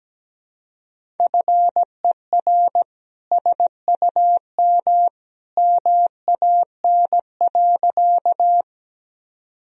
Morse code, heard as a keyed tone, about 17 wpm, 700 Hz. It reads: FER SUM MAN.
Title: Sentences from Top 200 Words and Abbreviations 17wpm